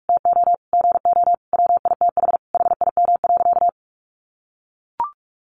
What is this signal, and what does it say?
Morse code, heard as a keyed tone, about 44 wpm, 700 Hz. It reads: TO GO WITH HIM.
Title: Sentences from Top 1300 Words 45wpm